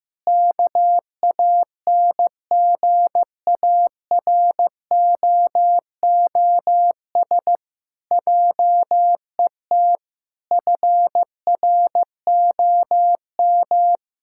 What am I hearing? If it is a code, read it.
Morse code, heard as a keyed tone, about 15 wpm, 700 Hz. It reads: KANGAROOS JET FROM
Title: Pangram Challenge v2 - 15wpm